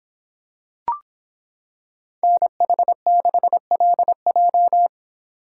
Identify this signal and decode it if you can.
Morse code, heard as a keyed tone, about 25 wpm, 700 Hz. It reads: NH6LJ